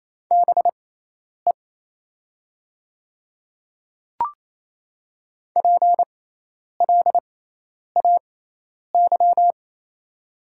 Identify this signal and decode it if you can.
Morse code, heard as a keyed tone, about 28 wpm, spaced as for 10 wpm, 700 Hz. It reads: BE PLAY
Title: 6x Character Spacing - Top 300 Words - 28wpm